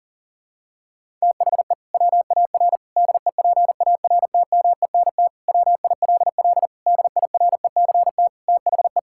Morse code, heard as a keyed tone, about 37 wpm, 700 Hz. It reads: THE WAR DEPARTMENT WILL DIRECT THE
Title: Sentences from Top 1000 Words - Review Part 5 - 40wpm